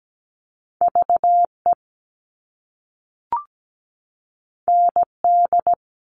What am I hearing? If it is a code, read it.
Morse code, heard as a keyed tone, about 17 wpm, 700 Hz. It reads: VE ND
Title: All Cut Numbers - 2-Digit 17wpm